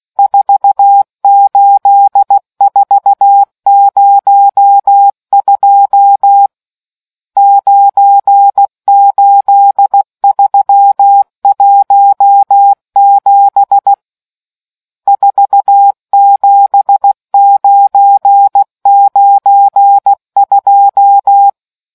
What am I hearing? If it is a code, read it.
Morse code, heard as a keyed tone, about 16 wpm, 800 Hz. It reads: 48402 98317 47992